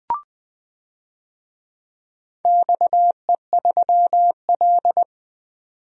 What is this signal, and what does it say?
Morse code, heard as a keyed tone, about 20 wpm, 700 Hz. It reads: XE3L
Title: DXCC-100 Call Signs - 4 Characters - 20wpm